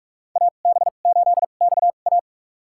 Morse code, heard as a keyed tone, about 41 wpm, spaced as for 31 wpm, 700 Hz. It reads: AB8XA
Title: USA Call Signs - 5 Characters - 2x Character Spacing - 45wpm